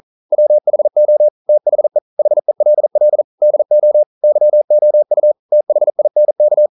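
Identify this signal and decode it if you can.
Morse code, heard as a keyed tone, about 41 wpm, 600 Hz. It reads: WHO THE HELL DO YOU THINK
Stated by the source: Morse code practice at 41 wpm, from CW med Gnister.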